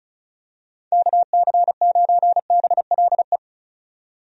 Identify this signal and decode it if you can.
Morse code, heard as a keyed tone, about 34 wpm, 700 Hz. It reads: KC9BLE
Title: USA Call Signs - 6 Characters - 35wpm